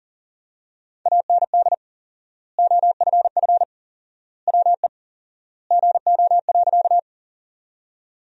Morse code, heard as a keyed tone, about 37 wpm, 700 Hz. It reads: AND OFF WE GO.
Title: 4x Word-Spacing - Sentences from Top 200 Words - 40wpm